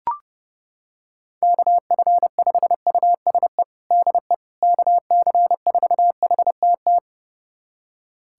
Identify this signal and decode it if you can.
Morse code, heard as a keyed tone, about 30 wpm, 700 Hz. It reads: KF5USE DE KC4HTT
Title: USA Callsign DE Callsign 30wpm